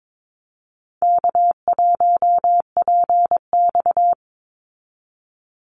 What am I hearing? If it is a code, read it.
Morse code, heard as a keyed tone, about 22 wpm, 700 Hz. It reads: K1PX
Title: Easy Call Signs Repeated 3x - 22wpm